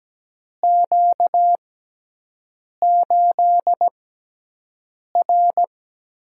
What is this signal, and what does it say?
Morse code, heard as a keyed tone, about 17 wpm, spaced as for 6 wpm, 700 Hz. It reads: Q8R
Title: USA Call Signs - 4 Characters - 6x Character Spacing - 17wpm